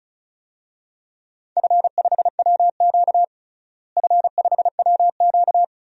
Morse code, heard as a keyed tone, about 36 wpm, 700 Hz. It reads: F5WQ F5WQ
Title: DXCC-100 Call Signs - 4 Characters - Repeated 3x - 35wpm